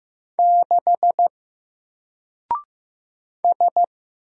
Morse code, heard as a keyed tone, about 15 wpm, 700 Hz. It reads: 6 S
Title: Single Letter-Number 15wpm